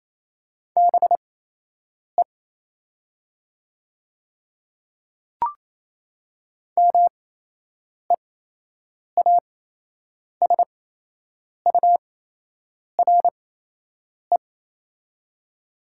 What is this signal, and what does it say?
Morse code, heard as a keyed tone, about 28 wpm, spaced as for 8 wpm, 700 Hz. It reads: BE MEASURE
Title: 8x Character Spacing - Top 500 Words - 28wpm